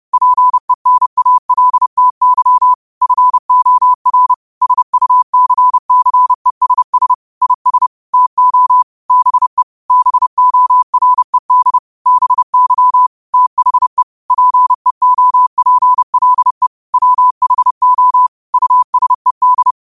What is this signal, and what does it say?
Morse code, heard as a keyed tone, about 30 wpm, 1000 Hz. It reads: PENALTY FOR SUCCESS IS TO BE BORED BY THE PEOPLE WHO USED
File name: qotdfiles / 30wpm